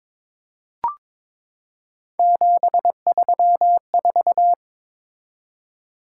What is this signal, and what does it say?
Morse code, heard as a keyed tone, about 21 wpm, 700 Hz. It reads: 734
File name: practice / 22wpm / Three Letter-Number Combinations - 22wpm